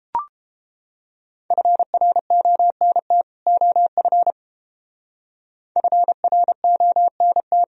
Morse code, heard as a keyed tone, about 29 wpm, 700 Hz. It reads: FRONT OF FRONT